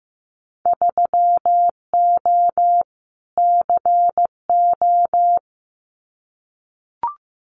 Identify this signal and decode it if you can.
Morse code, heard as a keyed tone, about 15 wpm, 700 Hz. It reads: 3O CO
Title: Winter Field Day Exchanges Encore 15wpm